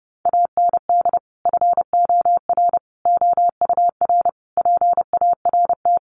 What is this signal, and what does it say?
Morse code, heard as a keyed tone, about 30 wpm, 700 Hz. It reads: AND FOR OUR PART
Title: Sentences from Top 1100 Words 30wpm